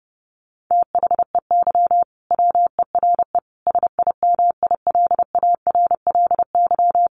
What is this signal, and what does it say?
Morse code, heard as a keyed tone, about 30 wpm, 700 Hz. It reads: THEY WERE SIMILARLY
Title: Sentences from Top 1400 Words 30wpm